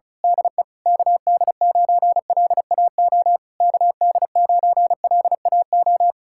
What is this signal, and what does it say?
Morse code, heard as a keyed tone, about 32 wpm, 700 Hz. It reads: DE KD9LAO KD9LAO
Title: Speed-Racing Common QSO Phrases - 35wpm